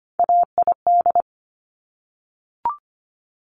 Morse code, heard as a keyed tone, about 25 wpm, 700 Hz. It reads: AID
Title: Top Three Letter Combinations in Words 25wpm